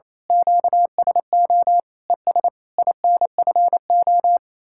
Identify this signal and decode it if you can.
Morse code, heard as a keyed tone, about 27 wpm, 700 Hz. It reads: QSO ES INFO
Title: Speed-Racing Common QSO Phrases Encore - 28wpm